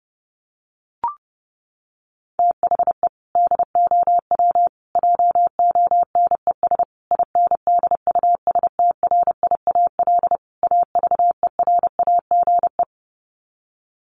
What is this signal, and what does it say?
Morse code, heard as a keyed tone, about 30 wpm, 700 Hz. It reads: THE DOW JONES INDUSTRIAL AVERAGE
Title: Sets of 5 Words 30wpm